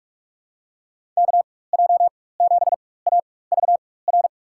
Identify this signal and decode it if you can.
Morse code, heard as a keyed tone, about 41 wpm, spaced as for 20 wpm, 700 Hz. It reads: KJ7AVR
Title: USA Call Signs - 6 Characters - 4x Character Spacing - 45wpm